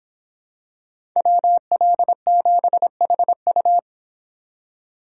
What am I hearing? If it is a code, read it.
Morse code, heard as a keyed tone, about 26 wpm, 700 Hz. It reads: WL7HU